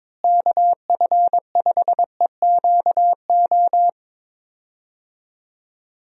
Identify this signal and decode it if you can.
Morse code, heard as a keyed tone, about 22 wpm, 700 Hz. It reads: KF5EQO
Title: USA Call Signs - 6 Characters - Repeated 2x - 22wpm